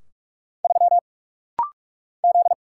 Morse code, heard as a keyed tone, about 46 wpm, 700 Hz. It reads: 3 Z
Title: Single Letter-Number Alt - Rapid-Fire 45wpm